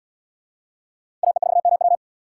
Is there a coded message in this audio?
Yes: Morse code from 1.2 to 1.9 s, heard as a keyed tone, about 73 wpm, 700 Hz. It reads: AE4MW